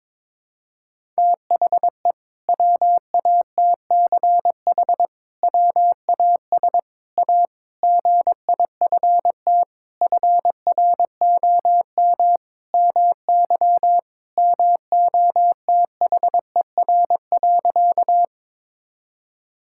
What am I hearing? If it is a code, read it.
Morse code, heard as a keyed tone, about 22 wpm, 700 Hz. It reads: THE WATCH WAS A GIFT FROM MY MOTHER.
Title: Sentences from Top 1000 Words - Review Part 6 - 22wpm